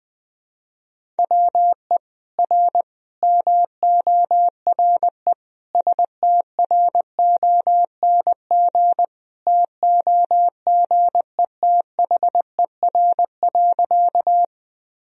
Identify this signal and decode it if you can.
Morse code, heard as a keyed tone, about 21 wpm, 700 Hz. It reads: WE R MORE STRONG TOGETHER.
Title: Sentences from Top 500 Words and Abbreviations 20wpm